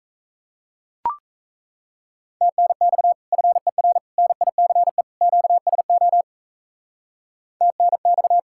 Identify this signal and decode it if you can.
Morse code, heard as a keyed tone, about 34 wpm, 700 Hz. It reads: TNX FER NICE QSO TNX
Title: Speed-Racing Common QSO Phrases - 28wpm